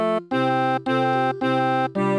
There is no code.